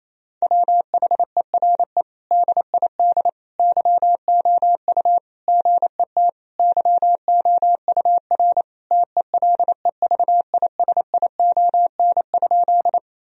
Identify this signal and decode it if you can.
Morse code, heard as a keyed tone, about 27 wpm, 700 Hz. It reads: WHERE DID YOU GET YOUR TELEVISION?
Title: Sentences from Top 1000 Words - Review Part 4 - 28wpm